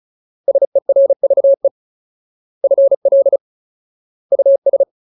Morse code, heard as a keyed tone, about 34 wpm, 550 Hz. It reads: SERVE FL US
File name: Ultimate Word List / 35WPM Ultimate